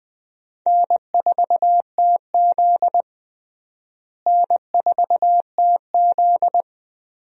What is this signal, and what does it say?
Morse code, heard as a keyed tone, about 20 wpm, 700 Hz. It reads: N4TZ N4TZ
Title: USA Call Signs - 4 Characters - Repeated 2x - 20wpm